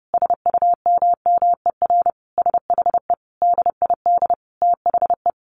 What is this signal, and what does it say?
Morse code, heard as a keyed tone, about 30 wpm, 700 Hz. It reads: SUMMER SHE DID THE
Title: Sentences from Top 1200 Words 30wpm